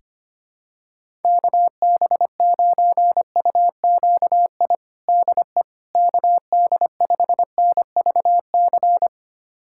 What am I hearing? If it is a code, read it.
Morse code, heard as a keyed tone, about 25 wpm, 700 Hz. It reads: KB9UQI DE KD5NVC
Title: USA Callsign DE Callsign - 6 Characters - 25wpm